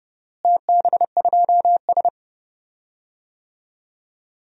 Morse code, heard as a keyed tone, about 30 wpm, 700 Hz. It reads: TB2S
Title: DXCC-100 Call Signs - 4 Characters - Repeated 3x - 30wpm